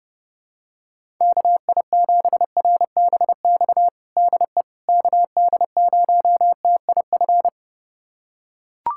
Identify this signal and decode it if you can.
Morse code, heard as a keyed tone, about 30 wpm, 700 Hz. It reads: KI7RBX DE KD0TIF
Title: USA Callsign DE Callsign - 6 Characters - 30wpm